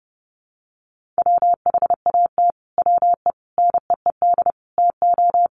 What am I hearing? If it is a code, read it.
Morse code, heard as a keyed tone, about 30 wpm, 700 Hz. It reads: WHAT WE NEED TO
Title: Sets of 5 Words Encore 30wpm